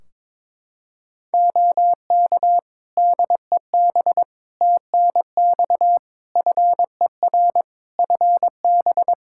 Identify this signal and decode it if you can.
Morse code, heard as a keyed tone, about 22 wpm, 700 Hz. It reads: OK DEB TNX FER FB
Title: Practice QSO - Exchange 4 - Ending - 22wpm